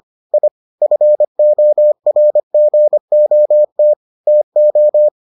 Morse code, heard as a keyed tone, about 25 wpm, 600 Hz. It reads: I FORGOT TO